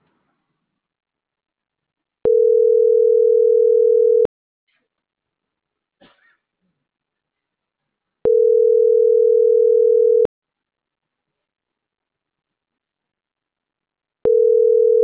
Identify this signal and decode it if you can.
Ringback tone (North America), 440 + 480 Hz.
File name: CENTRAL MAERSK / out gabrielle.melendez - .49385